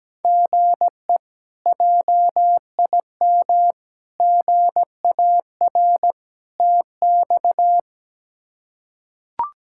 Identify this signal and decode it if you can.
Morse code, heard as a keyed tone, about 17 wpm, 700 Hz. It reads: GE JIM GAR TX
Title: K1USN SST QSO Practice - 17wpm